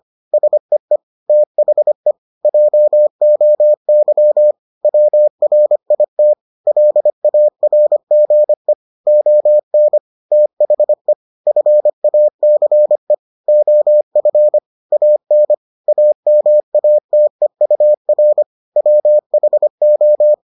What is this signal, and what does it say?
Morse code, heard as a keyed tone, about 25 wpm, 600 Hz. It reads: SEE THE JOY WRIT LARGE ON THE FACE OF AN AMATEUR WHO